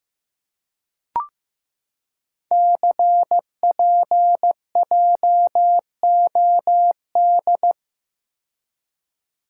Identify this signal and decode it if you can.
Morse code, heard as a keyed tone, about 15 wpm, 700 Hz. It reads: CPJOD
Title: Code Groups - 5 Letters - 15wpm